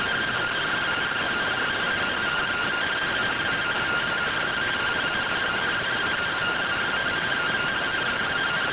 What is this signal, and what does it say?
Two-tone teleprinter signal, about 170 Hz shift.